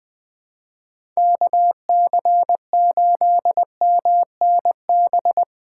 Morse code, heard as a keyed tone, about 20 wpm, 700 Hz. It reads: KC8MNB
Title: Speed Racing - USA 2x3 Call Signs 17wpm